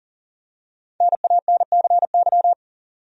Morse code, heard as a keyed tone, about 40 wpm, 700 Hz. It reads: NANCY
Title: USA Common Names - Max 5 Letters - 40wpm